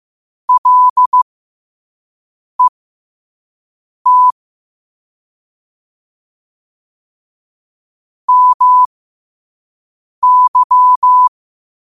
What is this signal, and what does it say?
Morse code, heard as a keyed tone, about 15 wpm, spaced as for 5 wpm, 1000 Hz. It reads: LET MY
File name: qotdfiles / 5wpm